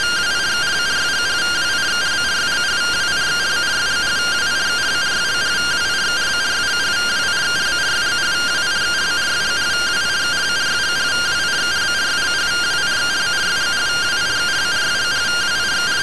Two-tone teleprinter signal, about 170 Hz shift.